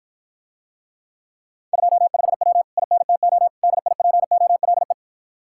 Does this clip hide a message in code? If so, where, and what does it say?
Yes: Morse code from 1.7 to 4.9 s, heard as a keyed tone, about 54 wpm, 700 Hz. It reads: 25W INTO DIPOLE